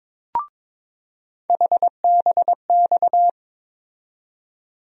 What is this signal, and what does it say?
Morse code, heard as a keyed tone, about 22 wpm, 700 Hz. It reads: HBX
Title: USA Call Sign Postfixes - 3 Letters 22wpm